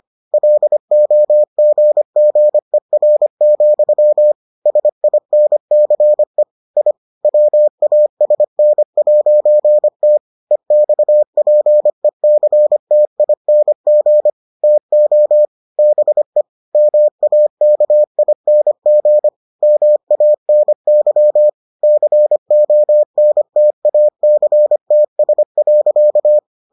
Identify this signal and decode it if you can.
Morse code, heard as a keyed tone, about 25 wpm, 600 Hz. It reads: LOGGER, SINCE I WASN'T EXPECTING TO BE MAKING MANY CONTACTS.